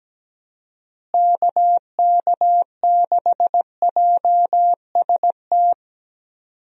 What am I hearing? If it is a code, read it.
Morse code, heard as a keyed tone, about 17 wpm, 700 Hz. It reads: KK6JST
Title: USA Call Signs - 6 Characters - Repeated 2x - 17wpm